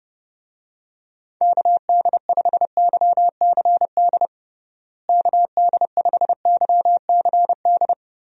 Morse code, heard as a keyed tone, about 30 wpm, 700 Hz. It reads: KD5YCD KD5YCD
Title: USA Call Signs - 6 Characters - Repeated 2x - 30wpm